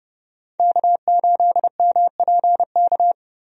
Morse code, heard as a keyed tone, about 30 wpm, 700 Hz. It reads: K8MPK
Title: USA Call Signs - 5 Characters - Repeated 2x - 30wpm